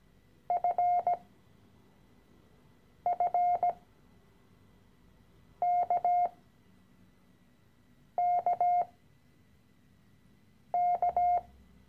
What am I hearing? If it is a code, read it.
Morse code, heard as a keyed tone, about 17 wpm, spaced as for 4 wpm, 710 Hz. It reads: FFKKK
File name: CW-Online-Kurs / Anfänger / Lektion 013